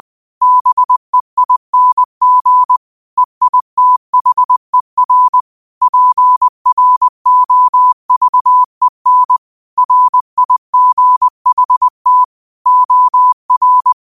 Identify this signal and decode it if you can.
Morse code, heard as a keyed tone, about 20 wpm, 1000 Hz. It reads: BEING EITHER PROVEN RIGHT OR
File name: qotdfiles / 20wpm